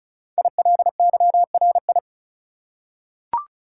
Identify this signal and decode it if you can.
Morse code, heard as a keyed tone, about 36 wpm, 700 Hz. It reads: ILYRI
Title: Code Groups - 5 Letters - 35wpm